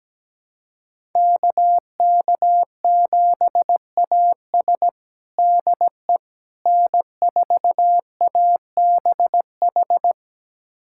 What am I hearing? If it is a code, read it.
Morse code, heard as a keyed tone, about 17 wpm, 700 Hz. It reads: KK7AS DE N4ABH
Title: USA Callsign DE Callsign - 5 Characters - 17wpm